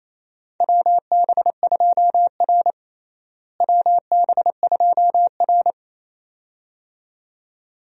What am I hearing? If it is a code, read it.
Morse code, heard as a keyed tone, about 28 wpm, 700 Hz. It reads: WB2R WB2R
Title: USA Call Signs - 4 Characters - Repeated 2x - 28wpm